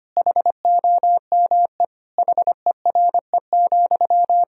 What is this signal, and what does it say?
Morse code, heard as a keyed tone, about 25 wpm, 700 Hz. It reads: HOME HERE,
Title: Sentences from Top 1000 Words - Review Part 1 - 25wpm